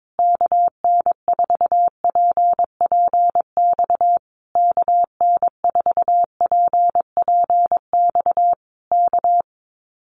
Morse code, heard as a keyed tone, about 22 wpm, 700 Hz. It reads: KN4PPX KN4PPX K
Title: Common QSO Phrases Encore 22wpm